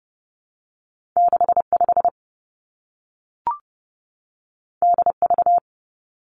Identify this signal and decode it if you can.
Morse code, heard as a keyed tone, about 30 wpm, 700 Hz. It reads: 6 5 D V
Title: Two Challenging Letter-Number Combinations 30wpm